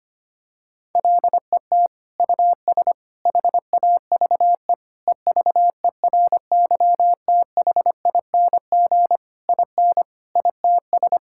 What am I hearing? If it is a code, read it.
Morse code, heard as a keyed tone, about 25 wpm, 700 Hz. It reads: LET US HAVE EVERYTHING IN ITS
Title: Sentences from Top 1000 Words - Review Part 6 - 25wpm